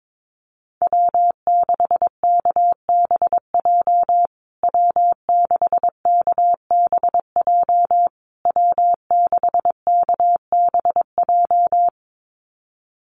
Morse code, heard as a keyed tone, about 22 wpm, 700 Hz. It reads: W6KBJ W6KBJ W6KBJ
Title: USA Call Signs Repeated 3x 22wpm